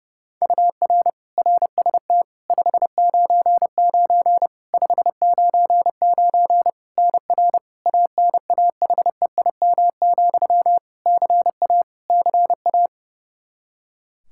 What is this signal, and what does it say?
Morse code, heard as a keyed tone, about 30 wpm, 700 Hz. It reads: UR RST 599 599 NR ANAHEIM, CA CA
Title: Practice QSO - Exchange 1 - Beginning - 30wpm